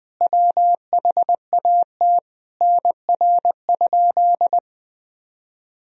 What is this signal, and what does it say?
Morse code, heard as a keyed tone, about 20 wpm, 700 Hz. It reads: WHAT NR?
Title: Sentences from Top 100 Words and Abbreviations 20wpm